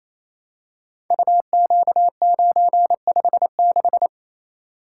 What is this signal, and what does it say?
Morse code, heard as a keyed tone, about 27 wpm, 700 Hz. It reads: UQ956
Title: Code Groups - 5 Letter:Numbers - 28wpm